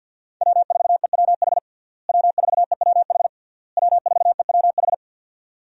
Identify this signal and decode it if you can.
Morse code, heard as a keyed tone, about 51 wpm, 700 Hz. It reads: W4EWH W4EWH W4EWH
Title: USA Call Signs - 5 Characters - Repeated 3x - 50wpm